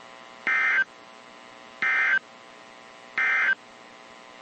Emergency Alert System (SAME) tones, heard as harsh buzzing bursts. It is heard from 0.3 to 3.6 s.
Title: WZ2523 - NOAA Weather Radio